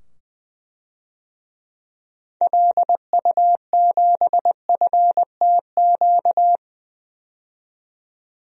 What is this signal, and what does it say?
Morse code, heard as a keyed tone, about 20 wpm, 700 Hz. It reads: LU7FTQ
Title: DXCC-100 Call Signs - 6 Characters - 20wpm